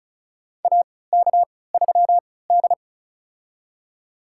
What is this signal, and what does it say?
Morse code, heard as a keyed tone, about 36 wpm, spaced as for 20 wpm, 700 Hz. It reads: AK3D
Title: USA Call Signs - 3x Character Spacing - 35wpm